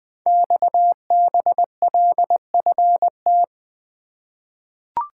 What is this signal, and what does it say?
Morse code, heard as a keyed tone, about 20 wpm, 700 Hz. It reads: XBLFT
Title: Code Groups - 5 Letters - 20wpm